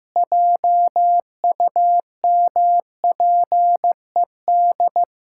Morse code, heard as a keyed tone, about 15 wpm, 700 Hz. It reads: JUMPED
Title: Pangram Challenge v2 - 15wpm